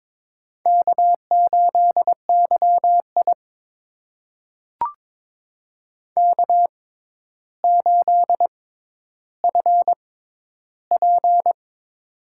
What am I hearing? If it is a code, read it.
Morse code, heard as a keyed tone, about 22 wpm, spaced as for 11 wpm, 700 Hz. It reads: K8YI K8FP
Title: USA Call Signs - 4 Characters - 6x Character Spacing - 22wpm